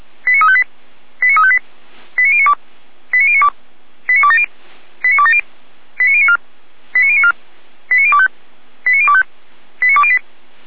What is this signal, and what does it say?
Five-tone selective call (ZVEI), heard as quick runs of five tones, tones of about 70 ms. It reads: 89358 89358 89092 89092 89270 89270 89094 89094 89025 89025 89208